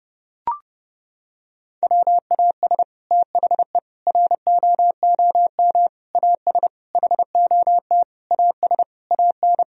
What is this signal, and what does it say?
Morse code, heard as a keyed tone, about 30 wpm, 700 Hz. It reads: WAS THE ROOM AS HOT AS AN
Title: Sentences from Top 1000 Words - Review Part 4 - 30wpm